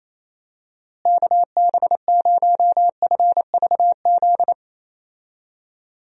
Morse code, heard as a keyed tone, about 28 wpm, 700 Hz. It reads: KB0FVZ
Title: USA Call Signs - 6 Characters - Repeated 3x - 28wpm